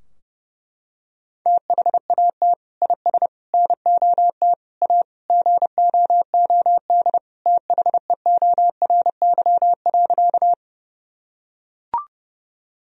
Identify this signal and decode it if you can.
Morse code, heard as a keyed tone, about 30 wpm, 700 Hz. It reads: THAT IS NOT A GOOD THEORY.
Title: Sentences from Top 1000 Words - Review Part 3 - 30wpm